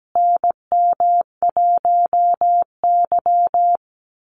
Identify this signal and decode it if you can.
Morse code, heard as a keyed tone, about 17 wpm, 700 Hz. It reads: NM1Y
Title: Call Signs Repeated 2x - 17wpm